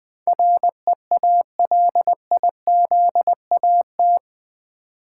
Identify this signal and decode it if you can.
Morse code, heard as a keyed tone, about 20 wpm, 700 Hz. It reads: REALIZAT
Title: Build A Word - 4 Letters At A Time - 11-Letter Words - 20wpm